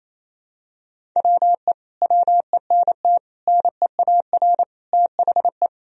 Morse code, heard as a keyed tone, about 27 wpm, 700 Hz. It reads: WE WENT NEAR THE
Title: Sentences from Top 1000 Words - Review Part 1 - 28wpm